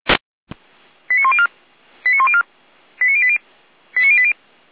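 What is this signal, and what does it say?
Five-tone selective call (ZVEI), heard as quick runs of five tones, tones of about 70 ms. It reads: 89104 89104 89080 89080
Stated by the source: ZVEI five-tone call received by SDR, from audio.f85360.feuerwehr-orlamuende.org fire brigade server.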